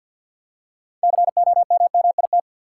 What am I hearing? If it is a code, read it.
Morse code, heard as a keyed tone, about 49 wpm, 700 Hz. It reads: COMMIT
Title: Six Letter Words - 50wpm